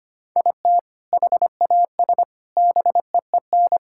Morse code, heard as a keyed tone, about 25 wpm, 700 Hz. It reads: IT HAS BEEN
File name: practice / 25wpm / Sentences with 3-Word Phrases - 25wpm